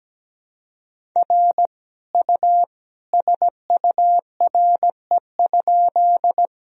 Morse code, heard as a keyed tone, about 17 wpm, 700 Hz. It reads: R U SURE?
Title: Sentences from Top 300 Words and Abbreviations 17wpm